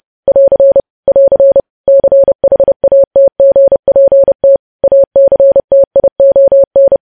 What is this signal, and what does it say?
Morse code, heard as a keyed tone, about 30 wpm, 550 Hz. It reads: <AR> <AR> CHATGPT ACTION
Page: news headlines in morse code at 30 wpm